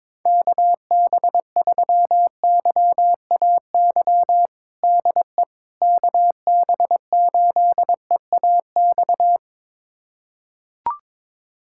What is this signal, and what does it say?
Morse code, heard as a keyed tone, about 22 wpm, 700 Hz. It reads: KB3YAY DE KB8EAX
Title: USA Callsign DE Callsign - 6 Characters - 22wpm